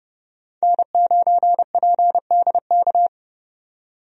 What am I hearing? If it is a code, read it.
Morse code, heard as a keyed tone, about 31 wpm, 700 Hz. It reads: N9PDK
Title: USA Call Signs - 5 Characters - 30wpm